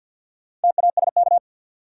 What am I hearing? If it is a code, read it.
Morse code, heard as a keyed tone, about 53 wpm, 700 Hz. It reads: TASK
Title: Sentences from Top 1000 Words - Review Part 7 - New Word Drill - 50wpm